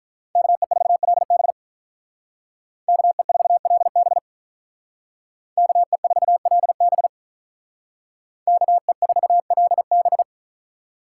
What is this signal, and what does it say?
Morse code, heard as a keyed tone, about 43 wpm, 700 Hz. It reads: KE4LB KE4LB KE4LB KE4LB
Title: Speed Racing - USA Call Signs 35wpm